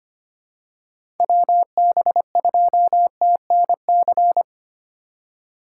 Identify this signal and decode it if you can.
Morse code, heard as a keyed tone, about 25 wpm, 700 Hz. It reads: WB2TNC
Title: USA Call Signs - 6 Characters - 3x Character Spacing - 25wpm